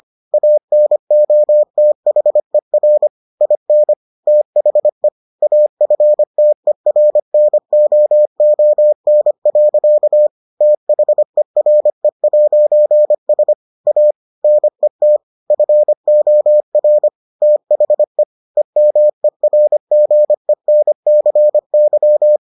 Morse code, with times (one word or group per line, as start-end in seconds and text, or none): 0.3-3.1 ANOTHER
3.4-3.9 IN
4.3-5.1 THE
5.4-10.3 AFTERNOON.
10.6-13.5 THERE'S
13.9-14.1 A
14.4-15.2 NET
15.5-17.1 FOR
17.4-18.2 THE
18.6-22.4 EMERGENCY